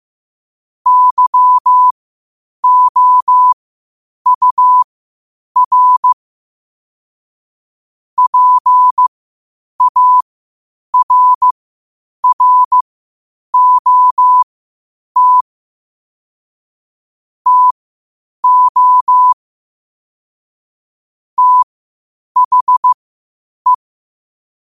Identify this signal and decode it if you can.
Morse code, heard as a keyed tone, about 15 wpm, spaced as for 8 wpm, 1000 Hz. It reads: YOUR PARROT TO THE
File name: qotdfiles / 8wpm